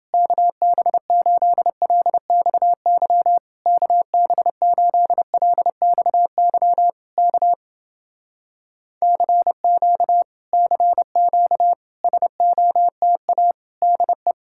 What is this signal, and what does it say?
Morse code, heard as a keyed tone, about 29 wpm, 700 Hz. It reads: KB8LXY KB8LXY K CQ CQ SOTA DE